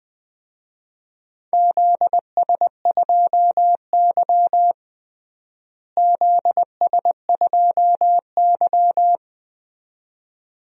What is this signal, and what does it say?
Morse code, heard as a keyed tone, about 20 wpm, 700 Hz. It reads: ZS2Y ZS2Y
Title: DXCC-100 Call Signs - 4 Characters - Repeated 2x - 20wpm